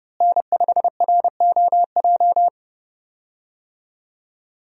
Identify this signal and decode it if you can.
Morse code, heard as a keyed tone, about 30 wpm, 700 Hz. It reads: N5ROJ
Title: USA Call Signs - 5 Characters - Repeated 2x - 30wpm